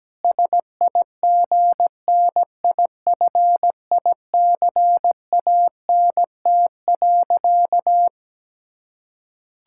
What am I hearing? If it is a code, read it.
Morse code, heard as a keyed tone, about 17 wpm, 700 Hz. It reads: SIGNIFICANT.
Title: Sentences from Top 1000 Words - Review Part 6 - 17wpm